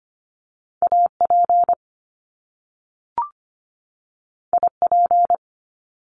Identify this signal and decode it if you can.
Morse code, heard as a keyed tone, about 25 wpm, 700 Hz. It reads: AP IP
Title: Top Two Letter Combinations in Words 25wpm